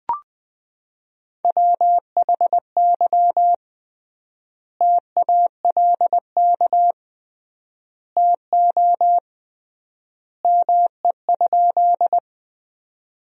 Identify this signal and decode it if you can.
Morse code, heard as a keyed tone, about 20 wpm, 700 Hz. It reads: WHY TALK TO ME?
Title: 3x Word-Spacing - Sentences from Top 400 Words - 20wpm